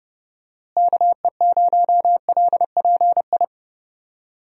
Morse code, heard as a keyed tone, about 30 wpm, 700 Hz. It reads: KE0LPI
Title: USA Call Signs - 6 Characters - Repeated 3x - 30wpm